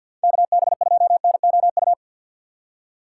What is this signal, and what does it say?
Morse code, heard as a keyed tone, about 47 wpm, 700 Hz. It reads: KB2NOU
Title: USA Call Signs - 6 Characters - 3x Character Spacing - 50wpm